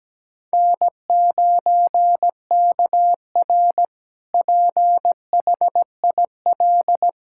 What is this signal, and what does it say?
Morse code, heard as a keyed tone, about 17 wpm, 700 Hz. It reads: N9KR PHIL
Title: K1USN SST QSO Practice - 17wpm